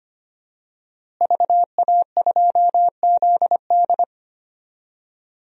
Morse code, heard as a keyed tone, about 25 wpm, 700 Hz. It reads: VA2ZD
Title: DXCC-100 Call Signs - 5 Characters - Repeated 3x - 25wpm